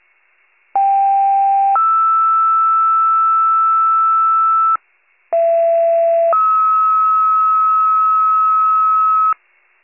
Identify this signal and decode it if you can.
Two-tone page, heard as a one-second tone, then a three-second tone, about 790 then 1320 Hz, about 670 then 1290 Hz. It is heard from 0.7 to 9.3 s.